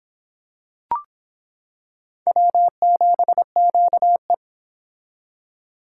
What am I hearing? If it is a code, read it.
Morse code, heard as a keyed tone, about 24 wpm, 700 Hz. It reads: W7QE